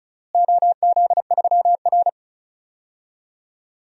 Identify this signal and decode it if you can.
Morse code, heard as a keyed tone, about 36 wpm, 700 Hz. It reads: OZ3R
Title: DXCC-100 Call Signs - 4 Characters - Repeated 2x - 35wpm